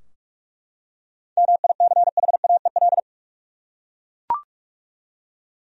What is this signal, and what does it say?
Morse code, heard as a keyed tone, about 41 wpm, 700 Hz. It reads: MICHAEL
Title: USA Common Names - 45wpm